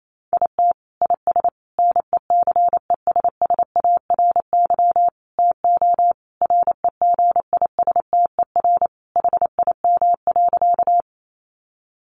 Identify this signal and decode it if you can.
Morse code, heard as a keyed tone, about 28 wpm, 700 Hz. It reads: IT IS NECESSARY TO REGISTER HIM.
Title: Sentences from Top 1200 Words 28wpm